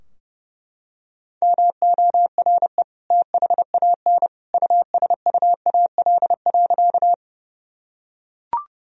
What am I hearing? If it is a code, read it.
Morse code, heard as a keyed tone, about 30 wpm, 700 Hz. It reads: MORE THAN USUAL.
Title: Sentences from Top 1000 Words - Review Part 2 - 30wpm